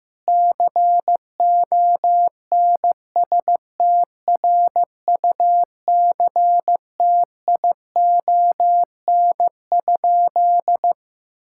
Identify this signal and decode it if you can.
Morse code, heard as a keyed tone, about 15 wpm, 700 Hz. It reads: CONSTRUCTION?
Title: Sentences from Top 1600 Words 15wpm